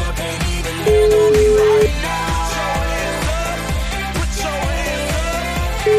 Ringback tone (Europe, 425 Hz), with noise.